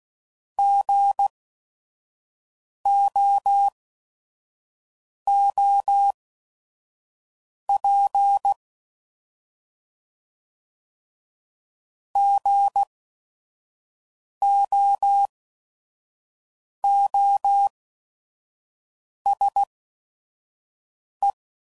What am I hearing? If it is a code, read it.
Morse code, heard as a keyed tone, about 16 wpm, spaced as for 5 wpm, 800 Hz. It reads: GOOP GOOSE
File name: Prefixes / By First Letter / Gs